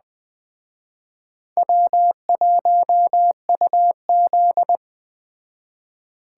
Morse code, heard as a keyed tone, about 21 wpm, 700 Hz. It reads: W1UZ
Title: Speed Racing - USA Call Signs 17wpm